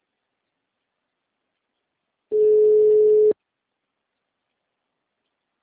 Ringback tone (Europe, 425 Hz).